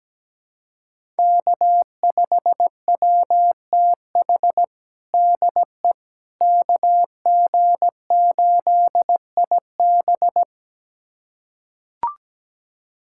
Morse code, heard as a keyed tone, about 17 wpm, 700 Hz. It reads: K5WTH DE KG8IB
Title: USA Callsign DE Callsign - 5 Characters - 17wpm